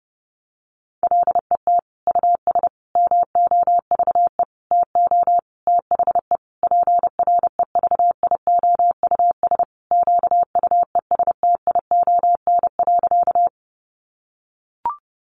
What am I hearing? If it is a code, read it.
Morse code, heard as a keyed tone, about 30 wpm, 700 Hz. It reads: LET US MOVE TO THE PREVIOUS QUESTION.
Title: Sentences from Top 1500 Words 30wpm